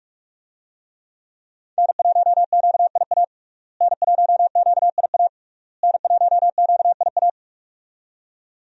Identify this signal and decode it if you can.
Morse code, heard as a keyed tone, about 41 wpm, 700 Hz. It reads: N1QIA N1QIA N1QIA
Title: USA Call Signs - 5 Characters - Repeated 3x - 45wpm